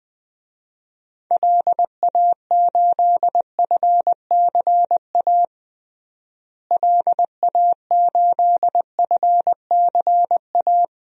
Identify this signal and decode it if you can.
Morse code, heard as a keyed tone, about 20 wpm, 700 Hz. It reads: LA8FCA LA8FCA
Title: DXCC-100 Call Signs - 6 Characters - Repeated 2x - 20wpm